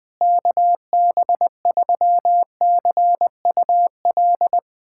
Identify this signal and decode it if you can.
Morse code, heard as a keyed tone, about 20 wpm, 700 Hz. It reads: KB3CUL
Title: USA Callsign DE Callsign 20wpm